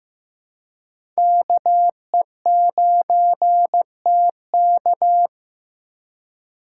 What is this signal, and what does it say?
Morse code, heard as a keyed tone, about 15 wpm, 700 Hz. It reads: KE9TK
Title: USA Call Signs - 5 Characters - 2x Character Spacing - 15wpm